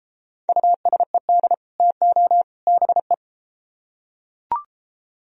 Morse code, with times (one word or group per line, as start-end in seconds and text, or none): 0.5-1.5 USED
1.8-2.4 TO
2.7-3.1 BE